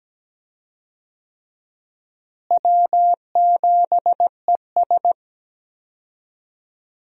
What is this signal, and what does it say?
Morse code, heard as a keyed tone, about 17 wpm, 700 Hz. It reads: W7ES
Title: USA Call Signs - 4 Characters - Repeated 2x - 17wpm